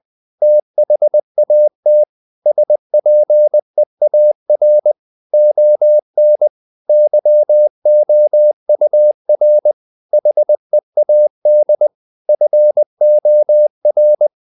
Morse code, heard as a keyed tone, about 20 wpm, 600 Hz. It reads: THAT SPEAR ON YOUR HEAD FOR
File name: the old man and the sea part 2 20wpm